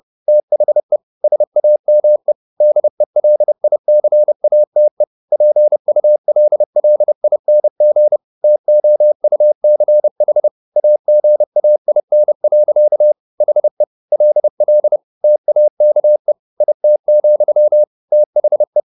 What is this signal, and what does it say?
Morse code, heard as a keyed tone, about 30 wpm, 600 Hz. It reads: THE SAME DELICATE PULLING TOUCH AGAIN. HE LL TAKE IT, THE
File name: the old man and the sea / the old man and the sea 1 30wpm